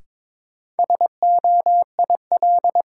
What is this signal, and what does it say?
Morse code, heard as a keyed tone, about 22 wpm, 700 Hz. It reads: SOIL
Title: Sentences from Top 1000 Words - Review Part 4 - New Word Drill - 22wpm